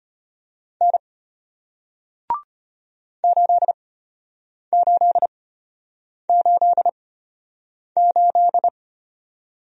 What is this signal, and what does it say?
Morse code, heard as a keyed tone, about 33 wpm, spaced as for 8 wpm, 700 Hz. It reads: N 8888